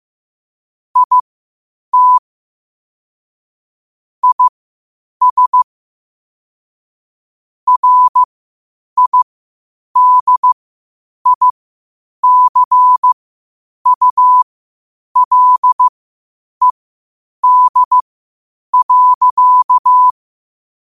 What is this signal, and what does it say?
Morse code, heard as a keyed tone, about 15 wpm, spaced as for 8 wpm, 1000 Hz. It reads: IT IS RIDICULED.
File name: qotdfiles / 8wpm